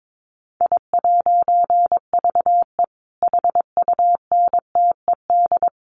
Morse code, heard as a keyed tone, about 22 wpm, 700 Hz. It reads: I'VE HUNTED
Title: A Sound Of Thunder 22wpm